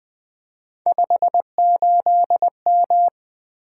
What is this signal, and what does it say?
Morse code, heard as a keyed tone, about 20 wpm, 700 Hz. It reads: 58M
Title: Three Letter-Number Combinations - 20wpm